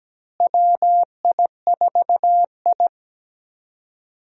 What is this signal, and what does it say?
Morse code, heard as a keyed tone, about 17 wpm, 700 Hz. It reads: WI4I
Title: USA Call Signs - 4 Characters - Repeated 3x - 17wpm